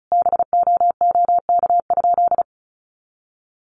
Morse code, heard as a keyed tone, about 34 wpm, 700 Hz. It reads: BOOK?
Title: Sentences from Top 1300 Words 35wpm